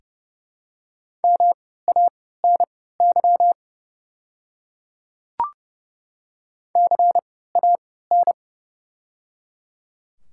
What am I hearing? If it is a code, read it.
Morse code, heard as a keyed tone, about 30 wpm, spaced as for 17 wpm, 700 Hz. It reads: MANY CAN
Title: 3x Character Spacing - Top 100 Words - 30wpm